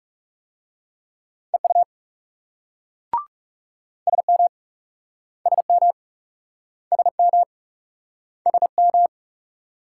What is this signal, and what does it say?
Morse code, heard as a keyed tone, about 42 wpm, 700 Hz. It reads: EU SM SM SM SM